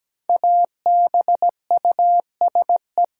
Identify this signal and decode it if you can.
Morse code, heard as a keyed tone, about 17 wpm, 700 Hz. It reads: ABUSE
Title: Sentences from Top 1300 Words - New Word Drill - 17wpm